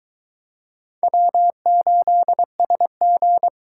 Morse code, heard as a keyed tone, about 23 wpm, 700 Hz. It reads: W8SG